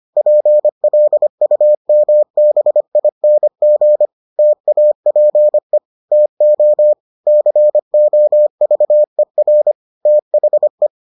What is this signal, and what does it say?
Morse code, heard as a keyed tone, about 25 wpm, 600 Hz. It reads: PLUMBING TAPE TO COVER THE